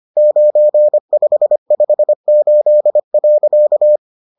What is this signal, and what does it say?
Morse code, heard as a keyed tone, about 25 wpm, 600 Hz. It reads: 9558.